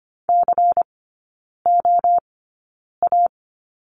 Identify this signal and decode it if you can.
Morse code, heard as a keyed tone, about 25 wpm, spaced as for 9 wpm, 700 Hz. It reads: COA
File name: course / 25:10wpm / 093 Lesson 16e - Letter c - Three Characters 25wpm